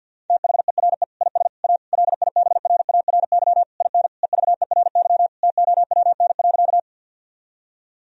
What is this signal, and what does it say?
Morse code, heard as a keyed tone, about 48 wpm, 700 Hz. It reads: THERE IS A LIBRARY IN EVERY TOWN.
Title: Sentences from Top 1000 Words - Review Part 4 - 50wpm